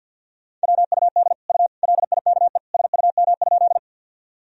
Morse code, heard as a keyed tone, about 51 wpm, 700 Hz. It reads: WUD U LIKE SUM?
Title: Sentences from Top 100 Words and Abbreviations 50wpm